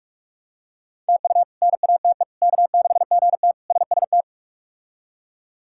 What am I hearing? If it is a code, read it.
Morse code, heard as a keyed tone, about 43 wpm, 700 Hz. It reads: TU NATE K6GT SST
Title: K1USN SST QSO Practice - 45wpm